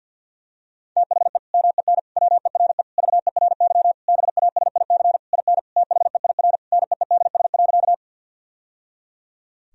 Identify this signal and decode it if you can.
Morse code, heard as a keyed tone, about 52 wpm, 700 Hz. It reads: THE MEN WERE VERY BASIC IN THEIR NEEDS.
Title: Sentences from Top 1000 Words - Review Part 3 - 50wpm